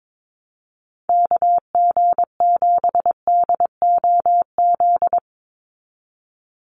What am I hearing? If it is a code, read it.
Morse code, heard as a keyed tone, about 22 wpm, 700 Hz. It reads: KG7DOZ
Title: USA Call Signs 22wpm